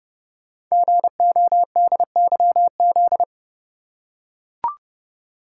Morse code, heard as a keyed tone, about 29 wpm, 700 Hz. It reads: GODYZ
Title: Code Groups - 5 Letters - 30wpm